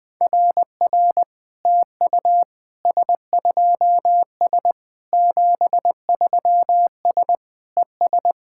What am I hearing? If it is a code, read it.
Morse code, heard as a keyed tone, about 21 wpm, 700 Hz. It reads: RR TU S2S 73S ES
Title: SOTA QSO Practice 20wpm